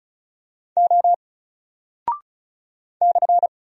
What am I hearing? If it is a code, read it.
Morse code, heard as a keyed tone, about 36 wpm, 700 Hz. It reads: O /
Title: Single Character (Letter:Number:Punctuation:Prosign) 35wpm